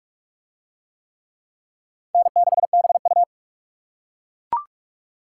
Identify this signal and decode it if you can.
Morse code, heard as a keyed tone, about 42 wpm, 700 Hz. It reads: N6BU